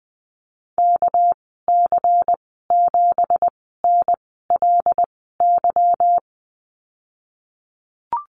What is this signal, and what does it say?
Morse code, heard as a keyed tone, about 20 wpm, spaced as for 15 wpm, 700 Hz. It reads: KC7NLY
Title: 213 Lesson 34g - Number 8 - Call Signs 20wpm